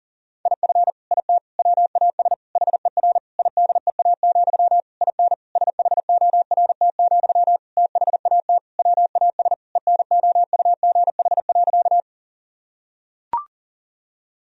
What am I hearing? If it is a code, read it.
Morse code, heard as a keyed tone, about 40 wpm, 700 Hz. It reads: IF IT WAS HER IDEA, IN SHORT, THAT WAS ENOUGH.
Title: Sentences from Top 1000 Words - Review Part 2 - 40wpm